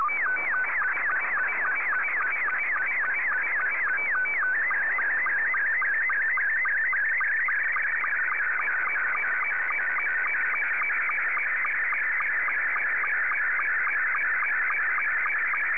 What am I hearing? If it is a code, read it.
SSTV (slow-scan television), Scottie 2.